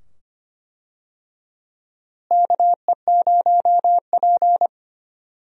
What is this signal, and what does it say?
Morse code, heard as a keyed tone, about 25 wpm, 700 Hz. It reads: KE0P
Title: USA Call Signs - 4 Characters - Repeated 2x - 25wpm